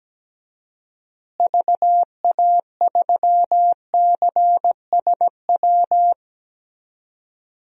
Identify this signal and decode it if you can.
Morse code, heard as a keyed tone, about 17 wpm, 700 Hz. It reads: VA3CSW
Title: DXCC-100 Call Signs - 6 Characters - Repeated 2x - 17wpm